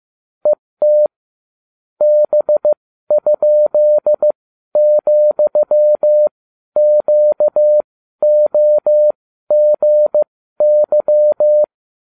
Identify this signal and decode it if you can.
Morse code, heard as a keyed tone, about 15 wpm, spaced as for 12 wpm, 600 Hz. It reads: ET B?,QOGY